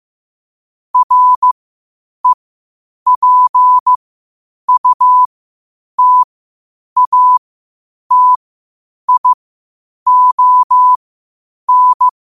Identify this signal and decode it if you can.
Morse code, heard as a keyed tone, about 15 wpm, spaced as for 8 wpm, 1000 Hz. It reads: REPUTATION
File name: qotdfiles / 8wpm